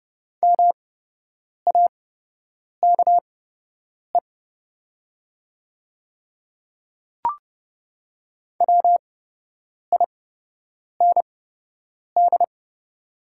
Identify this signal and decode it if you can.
Morse code, heard as a keyed tone, about 30 wpm, spaced as for 8 wpm, 700 Hz. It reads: MAKE WIND
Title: 8x Character Spacing - Top 500 Words - 30wpm